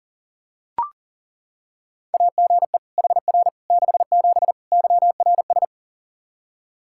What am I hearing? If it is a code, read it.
Morse code, heard as a keyed tone, about 40 wpm, 700 Hz. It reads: AGE HR 67 YRS
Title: K1USN SST QSO Practice 40wpm